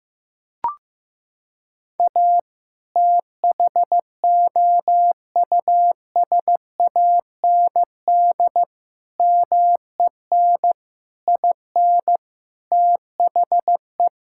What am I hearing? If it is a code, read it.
Morse code, heard as a keyed tone, about 15 wpm, 700 Hz. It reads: A THOUSAND MEN IN THE
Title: Sentences from Top 1000 Words - Review Part 6 - 15wpm